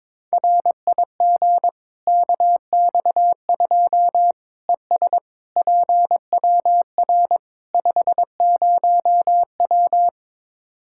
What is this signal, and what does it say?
Morse code, heard as a keyed tone, about 22 wpm, 700 Hz. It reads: RIG KX2 ES PWR 50W
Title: Practice QSO - Exchange 2 - Info - 22wpm